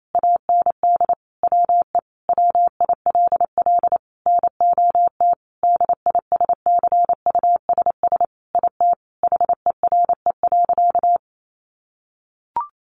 Morse code, heard as a keyed tone, about 28 wpm, 700 Hz. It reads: AND WE WILL NOT DISCUSS IT HERE.
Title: Sentences from Top 1100 Words 28wpm